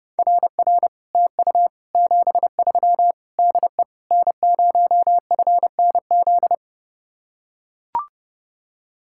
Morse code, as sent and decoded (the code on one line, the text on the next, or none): .-. .-. / - ..- / --... ...-- / -.. / -. ----- ..-. -. --..
RR TU 73 DE N0FNZ